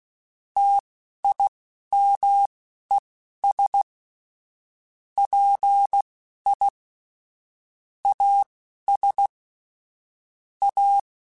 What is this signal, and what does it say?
Morse code, heard as a keyed tone, about 16 wpm, spaced as for 11 wpm, 800 Hz. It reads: TIMES PI AS A
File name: Technical Subjects / Technical Subjects-mp3s / XC @10